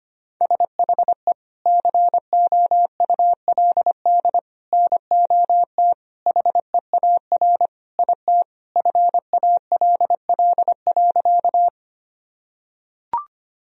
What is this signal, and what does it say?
Morse code, heard as a keyed tone, about 25 wpm, 700 Hz. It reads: SHE COULD NOT HEAR IT FALL.
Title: Sentences from Top 1000 Words - Review Part 2 - 25wpm